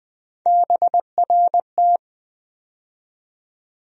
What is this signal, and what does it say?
Morse code, heard as a keyed tone, about 20 wpm, 700 Hz. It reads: BRT
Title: USA Call Sign Postfixes - 3 Letters 20wpm